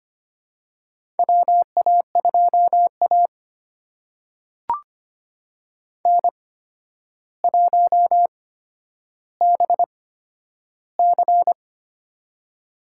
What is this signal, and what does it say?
Morse code, heard as a keyed tone, about 25 wpm, spaced as for 11 wpm, 700 Hz. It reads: WA2A N1BC
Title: USA Call Signs - 4 Characters - 8x Character Spacing - 25wpm